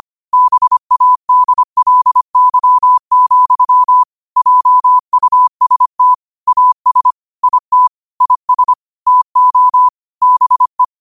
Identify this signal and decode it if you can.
Morse code, heard as a keyed tone, about 25 wpm, 1000 Hz. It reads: BADLY, JUST AS IT IS TO BE
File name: qotdfiles / 25wpm